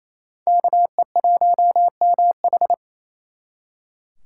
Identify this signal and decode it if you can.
Morse code, heard as a keyed tone, about 35 wpm, 700 Hz. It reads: KE1MH